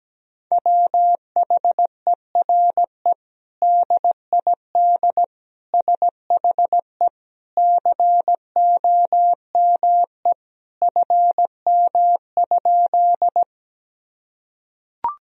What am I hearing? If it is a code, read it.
Morse code, heard as a keyed tone, about 17 wpm, 700 Hz. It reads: WHERE DID SHE COME FM?
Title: Sentences from Top 200 Words and Abbreviations 17wpm